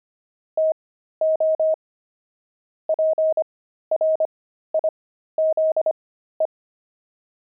Morse code, heard as a keyed tone, about 25 wpm, spaced as for 13 wpm, 630 Hz. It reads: TO PRIZE
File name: pangram-fs-13-cs-25-jolt